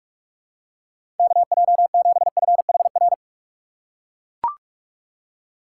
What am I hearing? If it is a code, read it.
Morse code, heard as a keyed tone, about 41 wpm, 700 Hz. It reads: KJ7FHR
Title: USA Call Signs - 6 Characters - 2x Character Spacing - 45wpm